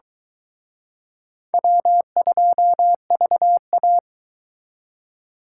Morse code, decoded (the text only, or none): W2VA